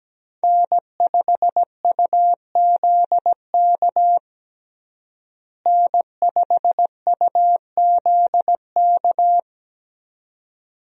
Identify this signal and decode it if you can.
Morse code, heard as a keyed tone, about 17 wpm, 700 Hz. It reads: N5UZK N5UZK
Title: USA Call Signs - 5 Characters - Repeated 2x - 17wpm